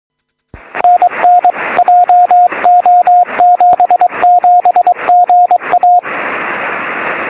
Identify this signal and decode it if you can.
Morse code, heard as a keyed tone, about 22 wpm, 710 Hz, with radio noise. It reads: NN JO77GA
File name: sm7lcb / 31-sm7gvf 23cm